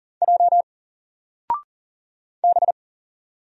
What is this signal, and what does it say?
Morse code, heard as a keyed tone, about 40 wpm, 700 Hz. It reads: J B
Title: Single Letter-Number Alt 40wpm